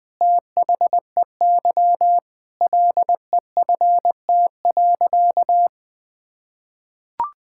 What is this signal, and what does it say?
Morse code, heard as a keyed tone, about 20 wpm, 700 Hz. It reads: THEY LEFT.
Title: Sentences with 3-Word Phrases - 20wpm